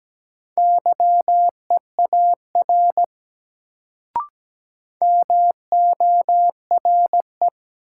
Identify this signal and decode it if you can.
Morse code, heard as a keyed tone, about 17 wpm, 700 Hz. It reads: YEAR MORE
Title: Sets of 2 Words - New Word Drill - 17wpm